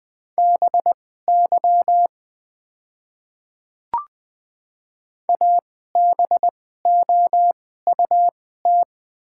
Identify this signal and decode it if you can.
Morse code, heard as a keyed tone, about 21 wpm, spaced as for 15 wpm, 700 Hz. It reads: BY ABOUT
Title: 2x Character Spacing - Top 100 Words - 20wpm